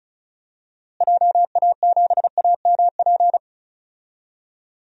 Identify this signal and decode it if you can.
Morse code, heard as a keyed tone, about 36 wpm, 700 Hz. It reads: JA7AMP
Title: DXCC-100 Call Signs - 6 Characters - Repeated 3x - 35wpm